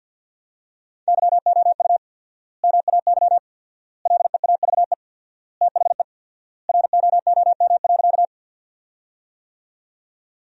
Morse code, heard as a keyed tone, about 50 wpm, 700 Hz. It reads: YOU MAY LEAVE THE ROOM.
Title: 4x Word-Spacing - Sentences from Top 400 Words - 50wpm